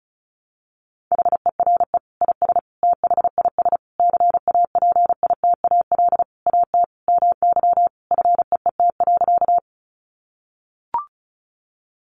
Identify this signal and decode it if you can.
Morse code, heard as a keyed tone, about 34 wpm, 700 Hz. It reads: HERE IS THIS CAPITAL AT MY FEET.
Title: Sentences from Top 1000 Words 35wpm